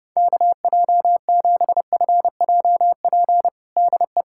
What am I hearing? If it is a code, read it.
Morse code, heard as a keyed tone, about 32 wpm, 700 Hz. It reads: KJ7FJP DE
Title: USA Callsign DE Callsign - 6 Characters - 30wpm